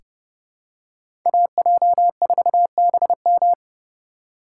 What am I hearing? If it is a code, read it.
Morse code, heard as a keyed tone, about 30 wpm, 700 Hz. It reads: AJ4BM
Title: USA Call Signs - 5 Characters - 3x Character Spacing - 30wpm